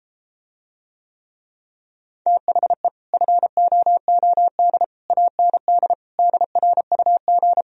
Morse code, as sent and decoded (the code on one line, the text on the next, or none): - .... / ..-. --- --- -.. / .- -. -.. / -.. .-. ..- --.
THE FOOD AND DRUG